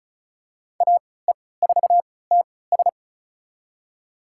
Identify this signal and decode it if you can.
Morse code, heard as a keyed tone, about 34 wpm, spaced as for 20 wpm, 700 Hz. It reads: AE4TS
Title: USA Call Signs - 3x Character Spacing - 35wpm